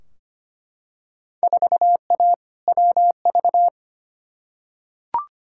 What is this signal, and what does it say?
Morse code, heard as a keyed tone, about 25 wpm, 700 Hz. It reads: 4A WV
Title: Field Day Exchanges Encore - 25wpm